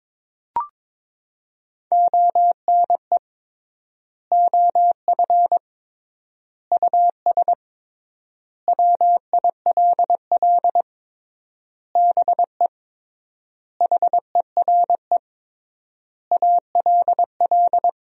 Morse code, heard as a keyed tone, about 22 wpm, 700 Hz. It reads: ONE OF US WILL BE HERE ALL
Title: 3x Word-Spacing - Sentences from Top 300 Words - 22wpm